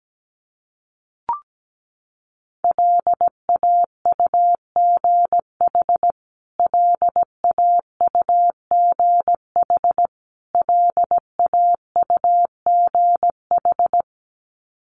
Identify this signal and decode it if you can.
Morse code, heard as a keyed tone, about 17 wpm, 700 Hz. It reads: LAUGH LAUGH LAUGH
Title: Top 500 Words Repeated 17wpm